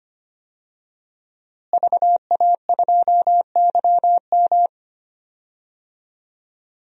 Morse code, heard as a keyed tone, about 25 wpm, 700 Hz. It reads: VA2YM
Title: DXCC-100 Call Signs - 5 Characters - Repeated 3x - 25wpm